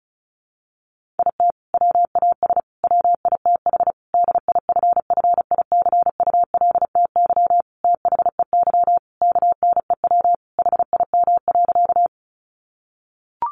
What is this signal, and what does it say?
Morse code, heard as a keyed tone, about 34 wpm, 700 Hz. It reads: IT WAS WITH DIFFICULTY THEY KNEW HIM.
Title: Sentences from Top 1500 Words 35wpm